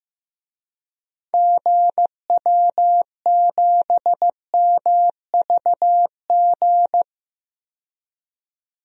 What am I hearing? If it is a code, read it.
Morse code, heard as a keyed tone, about 15 wpm, 700 Hz. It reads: GW7MVG
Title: DXCC-100 Call Signs - 6 Characters - 15wpm